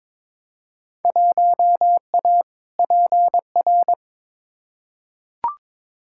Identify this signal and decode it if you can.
Morse code, heard as a keyed tone, about 22 wpm, 700 Hz. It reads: 1A PR
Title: Field Day Exchanges Encore - 22wpm